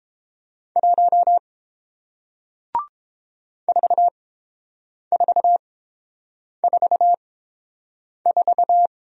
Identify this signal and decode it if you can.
Morse code, heard as a keyed tone, about 30 wpm, spaced as for 7 wpm, 700 Hz. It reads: 1 4444